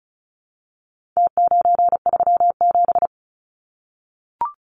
Morse code, heard as a keyed tone, about 34 wpm, 700 Hz. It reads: T937
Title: Numbers - Contest Style - 4-Digit 35wpm